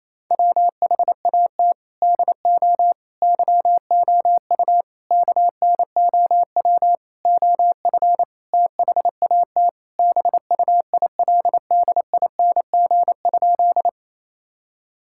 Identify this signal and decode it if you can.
Morse code, heard as a keyed tone, about 28 wpm, 700 Hz. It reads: WHAT DO YOU KNOW OF THAT BUILDING?
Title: Sentences from Top 1000 Words - Review Part 5 - 28wpm